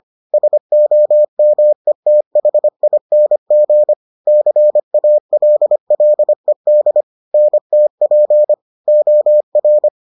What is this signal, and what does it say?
Morse code, heard as a keyed tone, about 25 wpm, 600 Hz. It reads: SOMETHING CALLED NTP OR